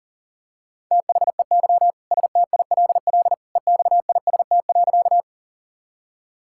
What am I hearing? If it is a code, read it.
Morse code, heard as a keyed tone, about 40 wpm, 700 Hz. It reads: THEY STILL EXIST.
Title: Sentences from Top 1000 Words - Review Part 5 - 40wpm